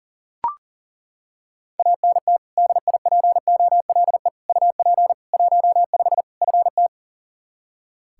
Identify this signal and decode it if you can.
Morse code, heard as a keyed tone, about 40 wpm, 700 Hz. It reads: ANT DIPOLE UP 15 FT
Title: Practice QSO - Exchange 2 - Info - 40wpm